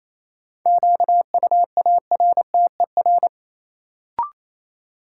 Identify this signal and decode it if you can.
Morse code, heard as a keyed tone, about 28 wpm, 700 Hz. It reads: QUARTER
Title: Sentences from Top 1300 Words - New Word Drill - 28wpm